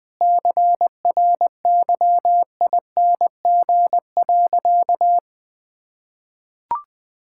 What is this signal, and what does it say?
Morse code, heard as a keyed tone, about 21 wpm, 700 Hz. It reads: CRYING.
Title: Sentences with 4-Word Phrases - 20wpm